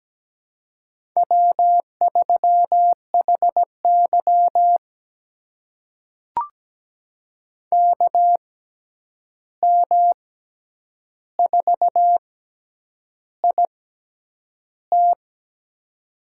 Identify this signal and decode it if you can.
Morse code, heard as a keyed tone, about 17 wpm, spaced as for 8 wpm, 700 Hz. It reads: W3HY KM4IT
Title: USA Call Signs - 6x Character Spacing - 17wpm